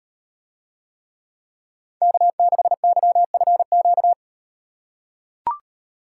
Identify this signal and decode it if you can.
Morse code, heard as a keyed tone, about 38 wpm, 700 Hz. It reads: K6YFQ